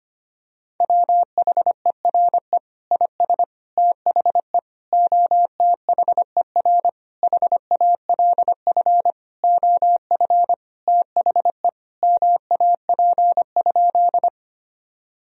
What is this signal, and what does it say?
Morse code, heard as a keyed tone, about 25 wpm, 700 Hz. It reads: WHERE IS THE OTHER HALF OF THE MAP?
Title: Sentences from Top 1000 Words - Review Part 2 - 25wpm